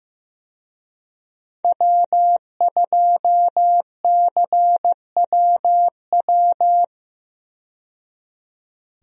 Morse code, heard as a keyed tone, about 15 wpm, 700 Hz. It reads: W2CWW
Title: USA Call Signs - 5 Characters - 15wpm